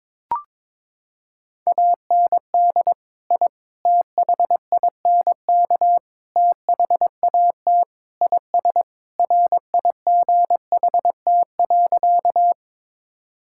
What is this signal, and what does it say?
Morse code, heard as a keyed tone, about 22 wpm, 700 Hz. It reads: AND I THINK THAT IS RIGHT.
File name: practice / 22wpm / Sentences with 4-Word Phrases - 22wpm